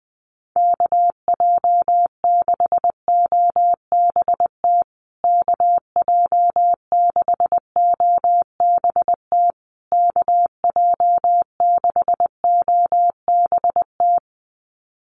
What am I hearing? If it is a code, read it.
Morse code, heard as a keyed tone, about 20 wpm, 700 Hz. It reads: KJ6OBT KJ6OBT KJ6OBT
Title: USA Call Signs Repeated 3x 20wpm